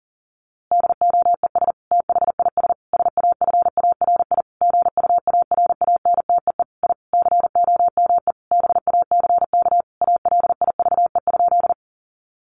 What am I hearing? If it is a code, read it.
Morse code, heard as a keyed tone, about 40 wpm, 700 Hz. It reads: DOES THIS SAFARI GUARANTEE I COME BACK ALIVE?
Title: A Sound Of Thunder 40wpm